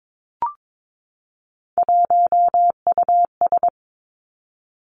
Morse code, heard as a keyed tone, about 22 wpm, 700 Hz. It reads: 1US
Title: Three Challenging Letter-Number Combinations 22wpm